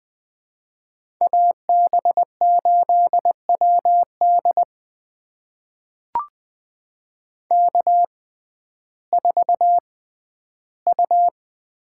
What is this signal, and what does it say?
Morse code, heard as a keyed tone, about 20 wpm, spaced as for 12 wpm, 700 Hz. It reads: AB8WD K4U
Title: USA Call Signs - 5 Characters - 6x Character Spacing - 20wpm